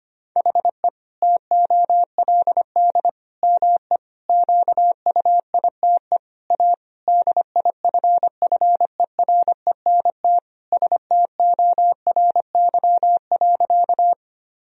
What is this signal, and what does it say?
Morse code, heard as a keyed tone, about 25 wpm, 700 Hz. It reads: HE TOLD ME QUITE A DIFFERENT STORY.
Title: Sentences from Top 1000 Words - Review Part 3 - 25wpm